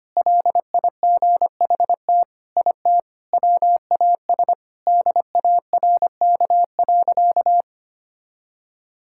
Morse code, heard as a keyed tone, about 25 wpm, 700 Hz. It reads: LIGHT IT WAS DARK.
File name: practice / 25wpm / Sentences with 4-Word Phrases - 25wpm